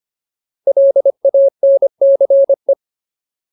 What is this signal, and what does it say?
Morse code, heard as a keyed tone, about 25 wpm, 550 Hz. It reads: LANCE
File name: Ultimate Word List / 25WPM Ultimate